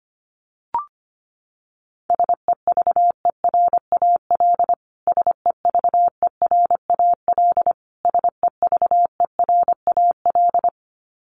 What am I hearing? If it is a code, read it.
Morse code, heard as a keyed tone, about 25 wpm, 700 Hz. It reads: SEVERAL SEVERAL SEVERAL
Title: Top 500 Words Repeated 25wpm